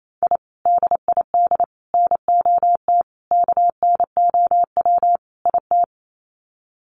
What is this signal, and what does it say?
Morse code, heard as a keyed tone, about 28 wpm, 700 Hz. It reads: I DID NOT KNOW IT
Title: Sets of 5 Words Encore2 28wpm